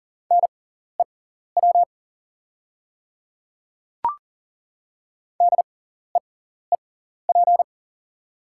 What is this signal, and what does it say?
Morse code, heard as a keyed tone, about 40 wpm, spaced as for 14 wpm, 700 Hz. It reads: NEW DEEP
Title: 6x Character Spacing - Top 500 Words - 40wpm